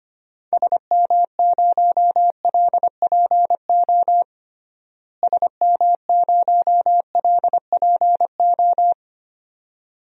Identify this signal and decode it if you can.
Morse code, heard as a keyed tone, about 25 wpm, 700 Hz. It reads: SM0LPO SM0LPO
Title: DXCC-100 Call Signs - 6 Characters - Repeated 2x - 25wpm